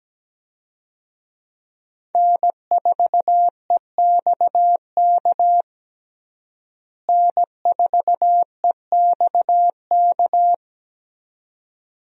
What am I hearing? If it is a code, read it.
Morse code, heard as a keyed tone, about 17 wpm, 700 Hz. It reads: N4EXK N4EXK
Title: USA Call Signs - 5 Characters - Repeated 2x - 17wpm